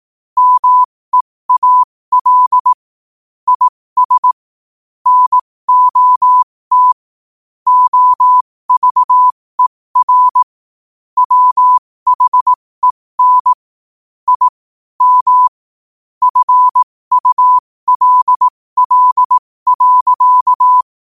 Morse code, heard as a keyed tone, about 18 wpm, spaced as for 15 wpm, 1000 Hz. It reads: MEAL IS NOT OVER WHEN I M FULL.
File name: qotdfiles / 15wpm